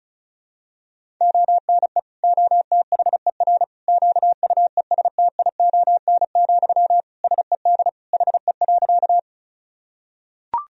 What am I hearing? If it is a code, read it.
Morse code, heard as a keyed tone, about 36 wpm, 700 Hz. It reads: ONE OTHER QUESTION, SED HE.
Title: Sentences from Top 400 Words and Abbreviations 35wpm